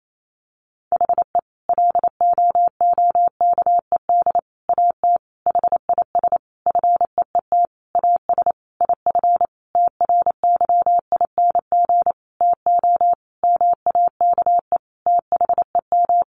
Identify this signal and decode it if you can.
Morse code, heard as a keyed tone, about 28 wpm, 700 Hz. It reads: HE LOOKED AT HIS FEET AS IF TRYING TO MAKE THEM
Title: A Sound Of Thunder 28wpm